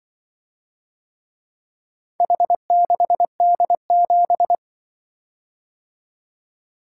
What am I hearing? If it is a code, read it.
Morse code, heard as a keyed tone, about 24 wpm, 700 Hz. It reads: H6D7
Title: Speed-Racing - Four Letter-Number Combinations - 20wpm